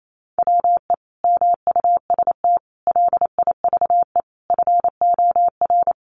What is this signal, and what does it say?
Morse code, heard as a keyed tone, about 28 wpm, 700 Hz. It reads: WE MUST LIVE FOR
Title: Sentences from Top 700 Words 28wpm